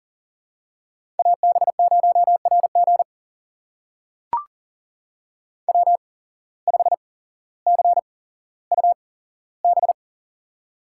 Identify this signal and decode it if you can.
Morse code, heard as a keyed tone, about 39 wpm, spaced as for 17 wpm, 700 Hz. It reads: AB0RG W5CUB
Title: USA Call Signs - 5 Characters - 8x Character Spacing - 40wpm